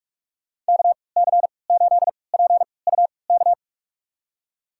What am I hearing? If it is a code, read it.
Morse code, heard as a keyed tone, about 43 wpm, spaced as for 25 wpm, 700 Hz. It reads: KC8PUK
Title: USA Call Signs - 6 Characters - 3x Character Spacing - 45wpm